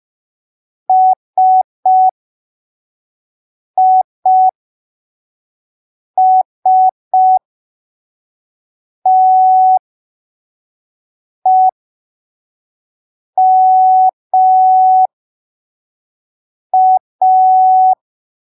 Morse code, heard as a keyed tone, about 5 wpm, spaced as for 3 wpm, 750 Hz. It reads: SISTEMA
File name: arquivos / texto / texto 22 05ppm